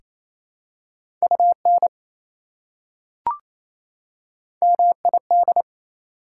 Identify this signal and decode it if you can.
Morse code, heard as a keyed tone, about 28 wpm, 700 Hz. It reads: UN MID
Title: Prefixes and Suffixes 28wpm